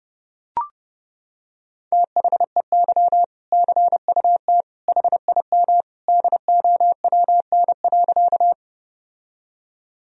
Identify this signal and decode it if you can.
Morse code, heard as a keyed tone, about 32 wpm, 700 Hz. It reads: THEY CUT HIM DOWN.
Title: Sentences from Top 1000 Words - Review Part 1 - 30wpm